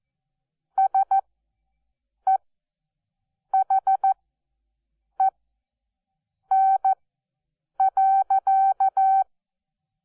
Morse code, heard as a keyed tone, about 14 wpm, spaced as for 6 wpm, 770 Hz. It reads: SEHEN.
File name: Onlinelehrgang / morsen / DJ4UF Morsen12